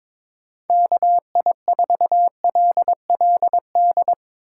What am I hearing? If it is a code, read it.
Morse code, heard as a keyed tone, about 22 wpm, 700 Hz. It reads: KI4LLD
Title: USA Call Signs - 6 Characters - 6x Character Spacing - 22wpm